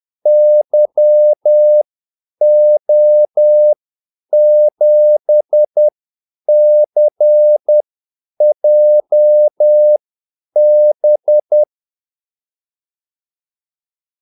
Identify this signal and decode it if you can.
Morse code, heard as a keyed tone, about 10 wpm, spaced as for 8 wpm, 600 Hz. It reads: YO7CJB
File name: Callsigns-10-8wpm